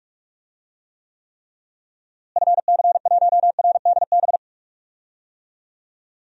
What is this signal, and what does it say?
Morse code, heard as a keyed tone, about 41 wpm, 700 Hz. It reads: FC1RDB
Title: DXCC-100 Call Signs - 6 Characters - Repeated 3x - 45wpm